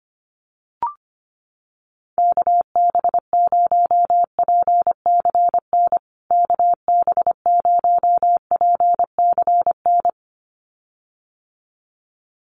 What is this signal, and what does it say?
Morse code, heard as a keyed tone, about 25 wpm, 700 Hz. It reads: KB0PCN KB0PCN
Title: USA Call Signs Repeated 2x 25wpm